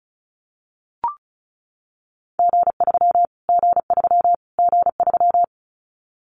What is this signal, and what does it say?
Morse code, heard as a keyed tone, about 34 wpm, 700 Hz. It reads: G3 G3 G3
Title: Callsign Prefixes for Top 25 DXCC - Repeated 3x - 35wpm